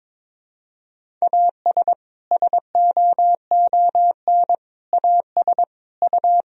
Morse code, heard as a keyed tone, about 22 wpm, 700 Hz. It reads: AS SOON AS U
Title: Sentences from Top 400 Words and Abbreviations 22wpm